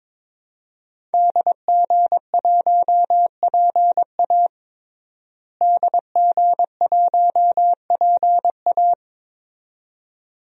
Morse code, heard as a keyed tone, about 22 wpm, 700 Hz. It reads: DG1PA DG1PA
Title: DXCC-100 Call Signs - 5 Characters - Repeated 2x - 22wpm